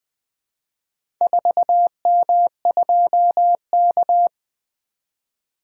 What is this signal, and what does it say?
Morse code, heard as a keyed tone, about 20 wpm, 700 Hz. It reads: 4M2K
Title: Four Letter-Number Combinations - 20wpm